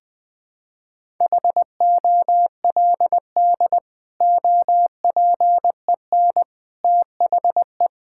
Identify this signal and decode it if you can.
Morse code, heard as a keyed tone, about 20 wpm, 700 Hz. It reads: HOLD OPEN THE
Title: Sentences from Top 1600 Words 20wpm